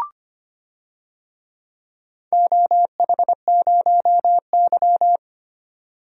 Morse code, heard as a keyed tone, about 25 wpm, 700 Hz. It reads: OH0Y
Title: DXCC-100 Call Signs - 4 Characters - 25wpm